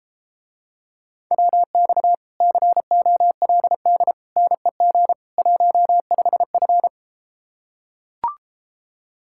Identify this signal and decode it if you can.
Morse code, heard as a keyed tone, about 33 wpm, 700 Hz. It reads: WX COLD NEG 15F